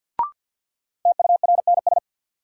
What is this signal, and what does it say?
Morse code, heard as a keyed tone, about 52 wpm, 700 Hz. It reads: TURNS
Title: Sets of 5 Words Encore - New Word Drill - 50wpm